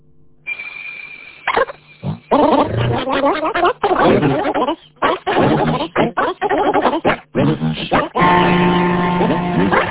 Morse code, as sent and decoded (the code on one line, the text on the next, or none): none